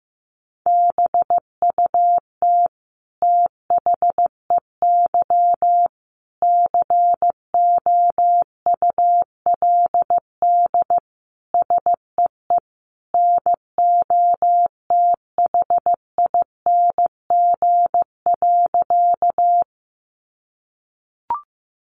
Morse code, heard as a keyed tone, about 15 wpm, 700 Hz. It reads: BUT THEY COULD SEE NOTHING.
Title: Sentences from Top 500 Words 15wpm